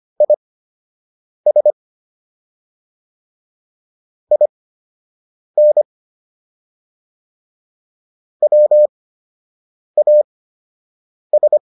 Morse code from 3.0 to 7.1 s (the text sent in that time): IN